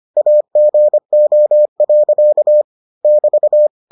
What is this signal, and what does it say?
Morse code, heard as a keyed tone, about 25 wpm, 600 Hz. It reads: AGO. <BT>